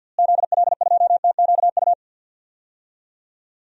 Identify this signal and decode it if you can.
Morse code, heard as a keyed tone, about 52 wpm, 700 Hz. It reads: ZL2TQU
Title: DXCC-100 Call Signs - 6 Characters - Repeated 2x - 50wpm